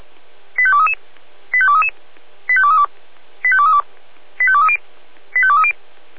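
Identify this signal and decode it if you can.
Five-tone selective call (ZVEI), heard as quick runs of five tones, tones of about 70 ms. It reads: 85233 85233 85232 85232 85230 85230
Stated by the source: ZVEI five-tone selective call 85852 received by SDR, fire brigade Orlamünde (feuerwehr-orlamuende).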